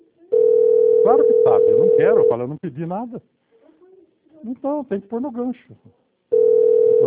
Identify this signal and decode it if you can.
Ringback tone (North America), 440 + 480 Hz, with speech.
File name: rg-1600-unknown- - - .26575